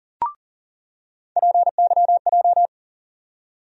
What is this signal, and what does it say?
Morse code, heard as a keyed tone, about 40 wpm, 700 Hz. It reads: PYJ
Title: USA Call Sign Postfixes - 3 Letters 40wpm